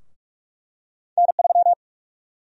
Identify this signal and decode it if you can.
Morse code, heard as a keyed tone, about 46 wpm, 700 Hz. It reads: N3
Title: USA Call Sign Prefixes - 45wpm